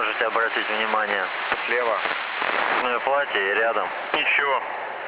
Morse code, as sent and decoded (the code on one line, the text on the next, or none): none